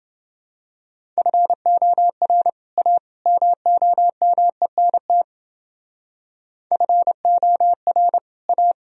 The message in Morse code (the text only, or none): FOR A MOMENT FOR A